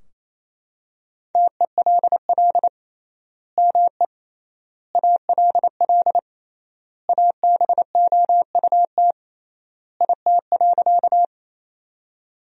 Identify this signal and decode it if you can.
Morse code, heard as a keyed tone, about 27 wpm, 700 Hz. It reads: TELL ME ALL ABOUT IT.
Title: 3x Word-Spacing - Sentences from Top 200 Words - 28wpm